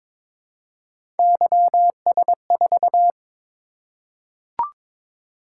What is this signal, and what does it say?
Morse code, heard as a keyed tone, about 22 wpm, 700 Hz. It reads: YS4
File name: practice / 22wpm / Three Letter-Number Combinations - 22wpm